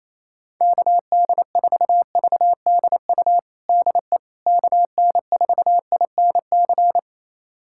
Morse code, heard as a keyed tone, about 28 wpm, 700 Hz. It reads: KD4VDU DE KN4INC
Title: USA Callsign DE Callsign - 6 Characters - 28wpm